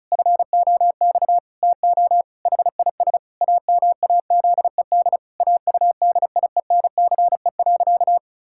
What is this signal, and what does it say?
Morse code, heard as a keyed tone, about 34 wpm, 700 Hz. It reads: FOX TO HIS AMAZED AUDIENCE.
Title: Pangram Challenge v2 - 35wpm